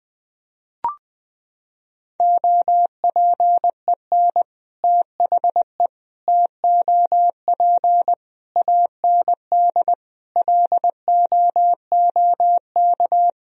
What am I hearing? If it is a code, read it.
Morse code, heard as a keyed tone, about 21 wpm, 700 Hz. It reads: OPEN THE TOP AND LOOK
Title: Sentences from Top 1000 Words - Review Part 2 - 20wpm